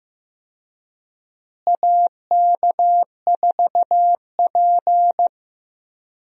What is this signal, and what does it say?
Morse code, heard as a keyed tone, about 15 wpm, 700 Hz. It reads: AK4P
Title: USA Call Signs - 4 Characters - Repeated 2x - 15wpm